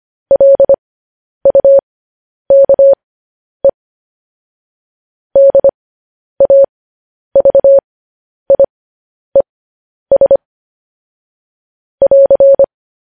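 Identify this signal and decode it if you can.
Morse code, heard as a keyed tone, about 25 wpm, spaced as for 10 wpm, 550 Hz. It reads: LUKE DAVIES <AR>
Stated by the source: Morse code news headlines at 10.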